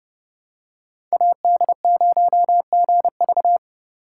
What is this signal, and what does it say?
Morse code, heard as a keyed tone, about 28 wpm, 700 Hz. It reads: AD0GV